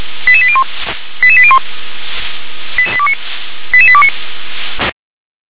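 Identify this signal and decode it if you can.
Five-tone selective call (ZVEI), heard as quick runs of five tones, tones of about 70 ms. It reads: 88801 88801 88829
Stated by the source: ZVEI five-tone selective call received on SDR.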